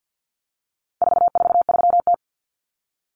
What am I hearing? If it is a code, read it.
Morse code, heard as a keyed tone, about 49 wpm, 700 Hz. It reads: 443T
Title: Common Cut Numbers - 4-Digit 50wpm